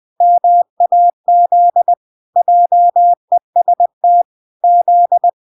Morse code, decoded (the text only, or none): MAZ JEST Z